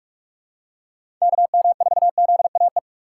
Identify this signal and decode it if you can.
Morse code, heard as a keyed tone, about 43 wpm, 700 Hz. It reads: KM4ZAE
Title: USA Call Signs - 2x Character Spacing - 45wpm